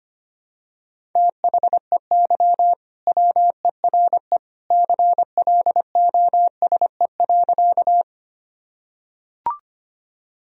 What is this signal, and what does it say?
Morse code, heard as a keyed tone, about 25 wpm, 700 Hz. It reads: THEY WERE CLOSE.
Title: Sentences from Top 1000 Words - Review Part 1 - 25wpm